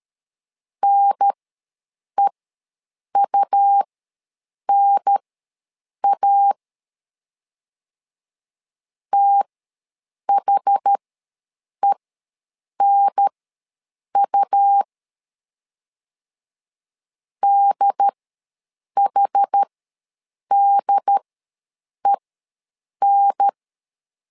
Morse code, heard as a keyed tone, about 13 wpm, spaced as for 7 wpm, 800 Hz. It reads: NEUNA THENU DHDEN
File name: LEZIONI / 2 UHDN / 2 UHDN ws 12wpm 1708